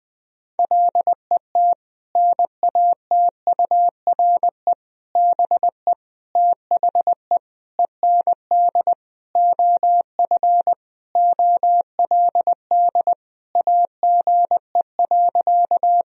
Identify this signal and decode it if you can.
Morse code, heard as a keyed tone, about 21 wpm, 700 Hz. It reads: LET NATURE BE THE END OF OLD AGE.
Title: Sentences from Top 1000 Words - Review Part 3 - 20wpm